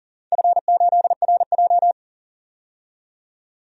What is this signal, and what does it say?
Morse code, heard as a keyed tone, about 40 wpm, 700 Hz. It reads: F8RJ
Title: DXCC-100 Call Signs - 4 Characters - 40wpm